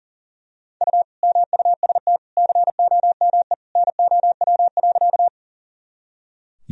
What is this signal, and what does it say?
Morse code, heard as a keyed tone, about 40 wpm, 700 Hz. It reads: U MUST COME NOW.
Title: Sentences from Top 200 Words and Abbreviations 40wpm